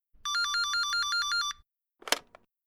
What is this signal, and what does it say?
There is no code.